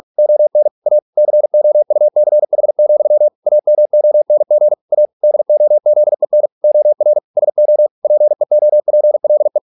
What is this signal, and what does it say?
Morse code, heard as a keyed tone, about 46 wpm, 600 Hz. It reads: ON A COUCH, AMONG A DOZEN OR SO PEOPLE